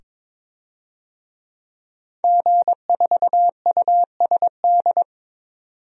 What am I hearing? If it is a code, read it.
Morse code, heard as a keyed tone, about 23 wpm, 700 Hz. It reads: G4USD
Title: DXCC-100 Call Signs - 5 Characters - 22wpm